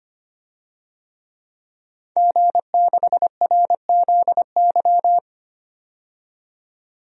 Morse code, heard as a keyed tone, about 25 wpm, 700 Hz. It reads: G6RZY
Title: DXCC-100 Call Signs - 5 Characters - 25wpm